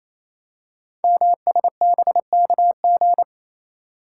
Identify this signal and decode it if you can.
Morse code, heard as a keyed tone, about 27 wpm, 700 Hz. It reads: MSBKG
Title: Code Groups - 5 Letters - 28wpm